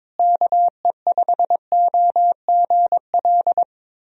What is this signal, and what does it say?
Morse code, heard as a keyed tone, about 22 wpm, 700 Hz. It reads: KE5OGL
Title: USA Call Signs - 6 Characters - Repeated 2x - 22wpm